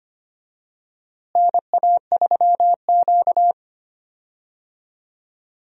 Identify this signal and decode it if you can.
Morse code, heard as a keyed tone, about 25 wpm, 700 Hz. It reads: NA3Q
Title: USA Call Signs - 4 Characters - Repeated 3x - 25wpm